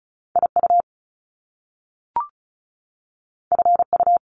Morse code, heard as a keyed tone, about 36 wpm, 700 Hz. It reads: IU FU
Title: Top Two Letter Combinations in Words 35wpm